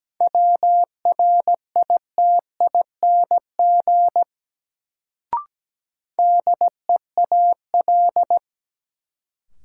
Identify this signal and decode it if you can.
Morse code, heard as a keyed tone, about 17 wpm, 700 Hz. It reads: WRITING DEAL
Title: Sentences from Top 800 Words - New Word Drill - 17wpm